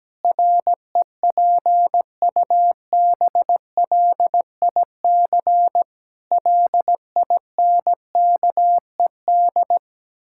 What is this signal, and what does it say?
Morse code, heard as a keyed tone, about 17 wpm, 700 Hz. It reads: REPUBLIC LINKED
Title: Pangram Challenge v2 - 17wpm